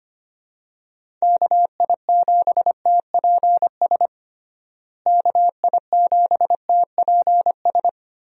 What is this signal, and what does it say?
Morse code, heard as a keyed tone, about 25 wpm, 700 Hz. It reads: KI7TPS KI7TPS
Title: USA Call Signs - 6 Characters - Repeated 2x - 25wpm